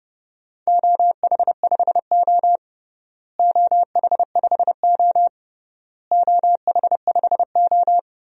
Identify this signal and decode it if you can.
Morse code, heard as a keyed tone, about 30 wpm, 700 Hz. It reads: OH5O OH5O OH5O
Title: DXCC-100 Call Signs - 4 Characters - Repeated 3x - 30wpm